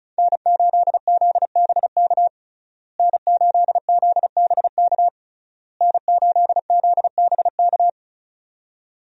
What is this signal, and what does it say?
Morse code, heard as a keyed tone, about 34 wpm, 700 Hz. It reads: N8ZBK N8ZBK N8ZBK
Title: USA Call Signs - 5 Characters - Repeated 3x - 35wpm